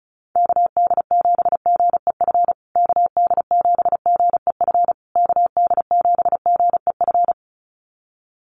Morse code, heard as a keyed tone, about 34 wpm, 700 Hz. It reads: KD7GEF KD7GEF KD7GEF
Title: USA Call Signs Repeated 3x 35wpm